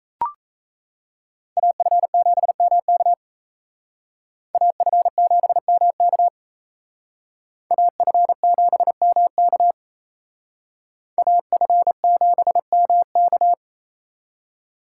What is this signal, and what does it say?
Morse code, heard as a keyed tone, about 35 wpm, 700 Hz. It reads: AF7MK AF7MK AF7MK AF7MK